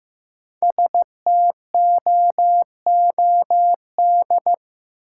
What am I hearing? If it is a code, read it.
Morse code, heard as a keyed tone, about 15 wpm, 700 Hz. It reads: STOOD
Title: Sentences from Top 500 Words - New Word Drill - 15wpm